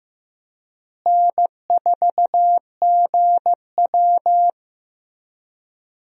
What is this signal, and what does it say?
Morse code, heard as a keyed tone, about 15 wpm, 700 Hz. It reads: N4GW
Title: USA Call Signs - 4 Characters - Repeated 2x - 15wpm